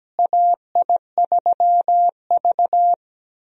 Morse code, heard as a keyed tone, about 17 wpm, 700 Hz. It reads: AI3V
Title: USA Call Signs - 4 Characters - Repeated 2x - 17wpm